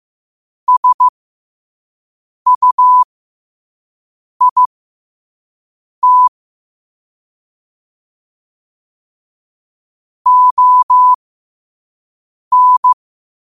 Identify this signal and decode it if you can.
Morse code, heard as a keyed tone, about 15 wpm, spaced as for 5 wpm, 1000 Hz. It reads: SUIT ON